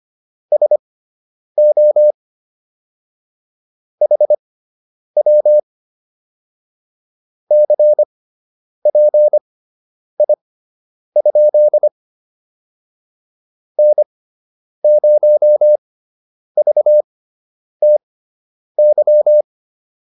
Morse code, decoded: SO HW CPI? N0VTY

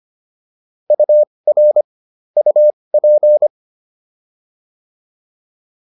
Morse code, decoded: UR UP

